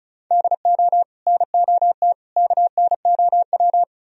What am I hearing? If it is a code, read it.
Morse code, heard as a keyed tone, about 46 wpm, spaced as for 41 wpm, 700 Hz. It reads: DO NOT KNOW
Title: Speed-Racing - Sets of 5 Words 35wpm